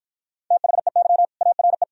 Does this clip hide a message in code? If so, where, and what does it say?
Yes: Morse code from 0.5 to 1.8 s, heard as a keyed tone, about 53 wpm, 700 Hz. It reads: THEY ARE